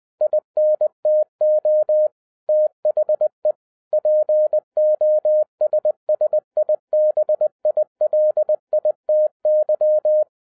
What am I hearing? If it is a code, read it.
Morse code, heard as a keyed tone, about 20 wpm, 610 Hz. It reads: INTO THE POSSIBILITY